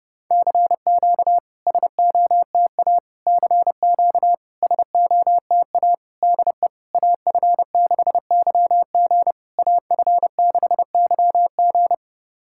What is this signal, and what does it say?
Morse code, heard as a keyed tone, about 30 wpm, 700 Hz. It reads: CQ SOTA CQ SOTA DE AF6YG AF6YG
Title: SOTA QSO Practice 30wpm